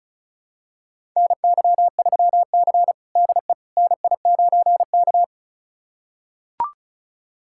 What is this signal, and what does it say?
Morse code, heard as a keyed tone, about 34 wpm, 700 Hz. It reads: NY3C DE NI9K
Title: USA Callsign DE Callsign - 4 Characters - 35wpm